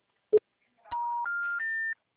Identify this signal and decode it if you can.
Special information tone.